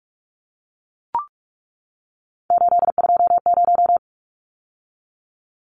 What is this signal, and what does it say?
Morse code, heard as a keyed tone, about 44 wpm, 700 Hz. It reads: 820
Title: Numbers - 3-Digit 45wpm